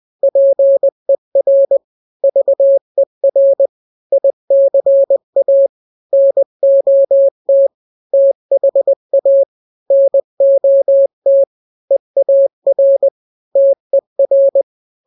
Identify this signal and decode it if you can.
Morse code, heard as a keyed tone, about 20 wpm, 550 Hz. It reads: PER VER ICA NOT THA NOT EAR TER
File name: n-grams / 3-grams 20 18 WPM